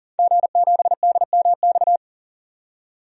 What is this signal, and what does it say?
Morse code, heard as a keyed tone, about 40 wpm, 700 Hz. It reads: G7DMX
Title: DXCC-100 Call Signs - 5 Characters - 40wpm